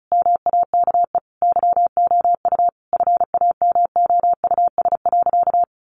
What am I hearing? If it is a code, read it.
Morse code, heard as a keyed tone, about 34 wpm, 700 Hz. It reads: MAKE YOU FAMOUS.
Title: Sentences from Top 1500 Words 35wpm